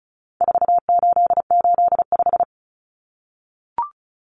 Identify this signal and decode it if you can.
Morse code, heard as a keyed tone, about 34 wpm, 700 Hz. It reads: 4885
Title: Common Cut Numbers - 4-Digit 35wpm